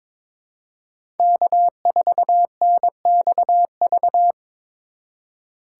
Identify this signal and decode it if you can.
Morse code, heard as a keyed tone, about 26 wpm, 700 Hz. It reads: K4NXV
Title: Speed Racing - USA 1x3 Call Signs 22wpm